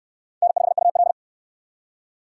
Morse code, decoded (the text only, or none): N5UL